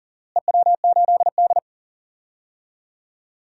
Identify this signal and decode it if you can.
Morse code, heard as a keyed tone, about 40 wpm, 700 Hz. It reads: EW8D
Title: DXCC-100 Call Signs - 4 Characters - Repeated 2x - 40wpm